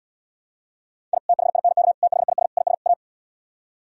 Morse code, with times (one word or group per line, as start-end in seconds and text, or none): 1.1-1.2 I
1.3-1.9 THINK
2.0-2.5 THIS
2.6-2.7 IS
2.9-2.9 A